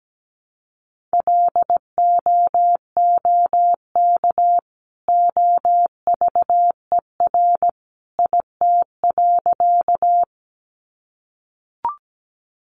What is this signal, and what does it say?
Morse code, heard as a keyed tone, about 17 wpm, 700 Hz. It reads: LOOK OVER IT.
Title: Sentences from Top 100 Words 17wpm